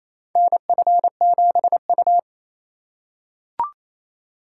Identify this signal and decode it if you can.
Morse code, heard as a keyed tone, about 28 wpm, 700 Hz. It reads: NF7U
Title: USA Call Signs - 4x Character Spacing - 28wpm